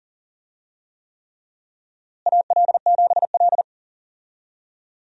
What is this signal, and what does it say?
Morse code, heard as a keyed tone, about 40 wpm, 700 Hz. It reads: AL7L
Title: USA Call Signs - 4 Characters - Repeated 3x - 40wpm